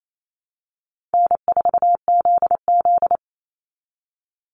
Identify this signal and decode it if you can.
Morse code, heard as a keyed tone, about 28 wpm, 700 Hz. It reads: N4ZZ
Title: Call Signs Repeated 3x - 28wpm